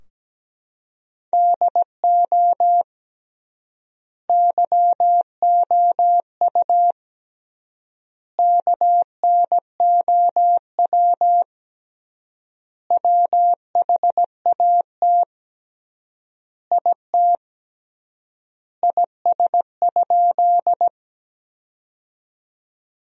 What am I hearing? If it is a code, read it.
Morse code, heard as a keyed tone, about 17 wpm, 700 Hz. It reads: DO YOU KNOW WHAT IT IS?
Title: 3x Word-Spacing - Sentences from Top 100 Words - 17wpm